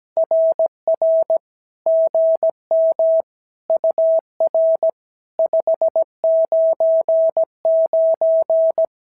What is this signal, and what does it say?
Morse code, heard as a keyed tone, about 17 wpm, 650 Hz. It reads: RR GM UR 599
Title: SOTA QSO Practice 17wpm